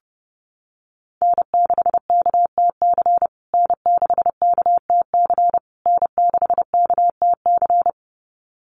Morse code, heard as a keyed tone, about 30 wpm, 700 Hz. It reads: N6KTC N6KTC N6KTC
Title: USA Call Signs Repeated 3x 30wpm